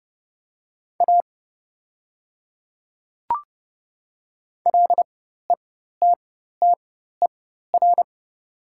Morse code, heard as a keyed tone, about 30 wpm, spaced as for 14 wpm, 700 Hz. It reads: A LETTER
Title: 4x Character Spacing - Top 300 Words - 30wpm